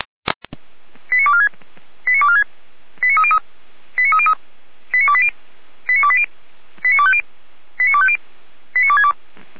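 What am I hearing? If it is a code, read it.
Five-tone selective call (ZVEI), heard as quick runs of five tones, tones of about 70 ms. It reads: 89246 89246 89303 89303 89280 89280 89250 89250 89262
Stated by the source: ZVEI selective call received by SDR, recorded by fire brigade Orlamünde (feuerwehr-orlamuende.org).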